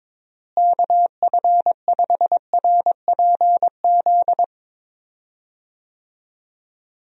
Morse code, heard as a keyed tone, about 21 wpm, 700 Hz. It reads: KF5RPZ
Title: USA Call Signs - 6 Characters - Repeated 2x - 22wpm